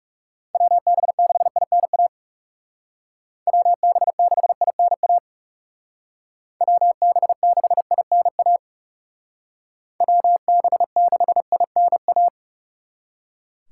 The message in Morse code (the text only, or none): WB6INA WB6INA WB6INA WB6INA